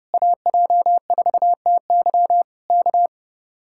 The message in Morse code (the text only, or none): AJ4TY K